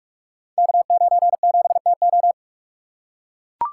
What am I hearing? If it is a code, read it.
Morse code, heard as a keyed tone, about 44 wpm, 690 Hz. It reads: K97TO
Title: Code Groups - 5 Letter:Numbers - 45wpm